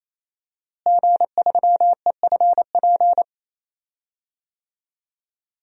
Morse code, heard as a keyed tone, about 28 wpm, 700 Hz. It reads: G3EFP
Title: DXCC-100 Call Signs - 5 Characters - Repeated 3x - 28wpm